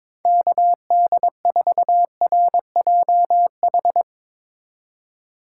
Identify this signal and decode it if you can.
Morse code, heard as a keyed tone, about 22 wpm, 700 Hz. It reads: KD4RJH
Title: Speed Racing - USA 2x3 Call Signs 22wpm